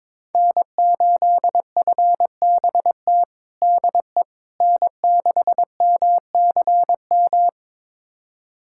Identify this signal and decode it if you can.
Morse code, heard as a keyed tone, about 22 wpm, 700 Hz. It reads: N8FBT DE N6MCM
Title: USA Callsign DE Callsign - 5 Characters - 22wpm